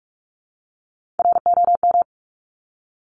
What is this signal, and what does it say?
Morse code, heard as a keyed tone, about 46 wpm, 700 Hz. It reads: ROM
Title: Top Three Letter Combinations in Words 45wpm